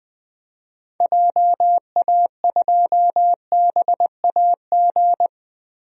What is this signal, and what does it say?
Morse code, heard as a keyed tone, about 19 wpm, 700 Hz. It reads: JA2BAG
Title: DXCC-100 Call Signs - 6 Characters - Repeated 3x - 20wpm